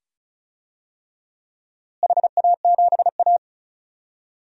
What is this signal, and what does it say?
Morse code, heard as a keyed tone, about 36 wpm, 700 Hz. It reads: HA7A
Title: DXCC-100 Call Signs - 4 Characters - Repeated 2x - 35wpm